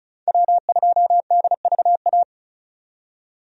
Morse code, heard as a keyed tone, about 34 wpm, 700 Hz. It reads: W2DVA
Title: USA Call Signs - 5 Characters - 35wpm